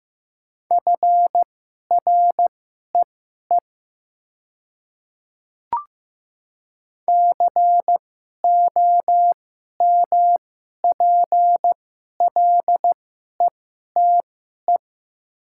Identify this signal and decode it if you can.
Morse code, heard as a keyed tone, about 15 wpm, spaced as for 11 wpm, 700 Hz. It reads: FREE COMPLETE
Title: 2x Character Spacing - Top 500 Words - 15wpm